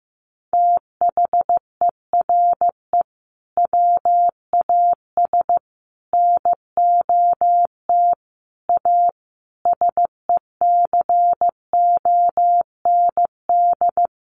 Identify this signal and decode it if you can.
Morse code, heard as a keyed tone, about 15 wpm, 700 Hz. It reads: THERE WAS NOT A SECOND
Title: Sentences from Top 1100 Words 15wpm